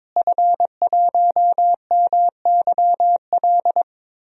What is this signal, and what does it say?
Morse code, heard as a keyed tone, about 23 wpm, 700 Hz. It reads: F1MYL
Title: DXCC-100 Call Signs - 5 Characters - Repeated 3x - 22wpm